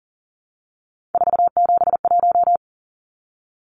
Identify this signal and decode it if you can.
Morse code, heard as a keyed tone, about 40 wpm, 700 Hz. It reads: 471
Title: Numbers - 3-Digit 40wpm